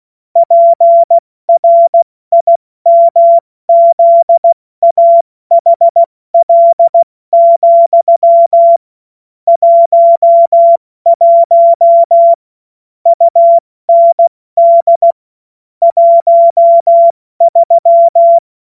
Morse code, heard as a keyed tone, about 16 wpm, 680 Hz. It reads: PRIMZAHL, 11 UND 13
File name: Morsekurs / Witze / Witz 12. April 2022 70 F80